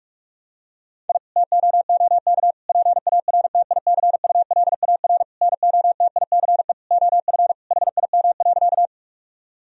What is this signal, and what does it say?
Morse code, heard as a keyed tone, about 46 wpm, 700 Hz. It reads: I TOOK PARTICULAR NOTICE OF HIM.
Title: Sentences from Top 1000 Words - Review Part 4 - 45wpm